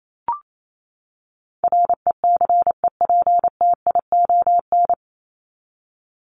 Morse code, heard as a keyed tone, about 28 wpm, 700 Hz. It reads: RECEPTION
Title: Top 1751-2000 Words 28wpm